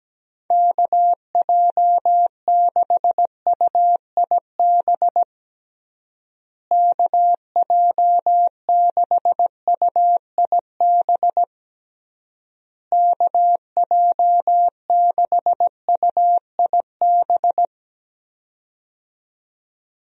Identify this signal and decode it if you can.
Morse code, heard as a keyed tone, about 17 wpm, 700 Hz. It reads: KJ6UIB KJ6UIB KJ6UIB
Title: USA Call Signs - 6 Characters - Repeated 3x - 17wpm